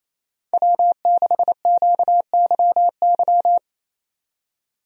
Morse code, heard as a keyed tone, about 27 wpm, 700 Hz. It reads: W6QYY
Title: USA Call Signs - 5 Characters - 28wpm